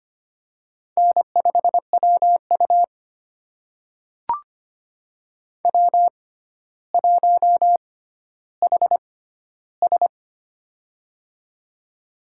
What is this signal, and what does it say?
Morse code, heard as a keyed tone, about 25 wpm, spaced as for 13 wpm, 700 Hz. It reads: N5WU W1HS
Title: USA Call Signs - 6x Character Spacing - 25wpm